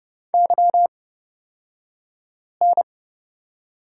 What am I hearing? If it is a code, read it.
Morse code, heard as a keyed tone, about 30 wpm, spaced as for 5 wpm, 700 Hz. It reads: YN
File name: Lesson 37d - Letter x - Two Characters 30wpm